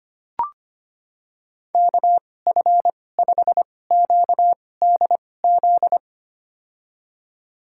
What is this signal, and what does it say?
Morse code, heard as a keyed tone, about 25 wpm, spaced as for 18 wpm, 700 Hz. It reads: KF5QDZ
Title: USA Call Signs - 6 Characters - 2x Character Spacing - 25wpm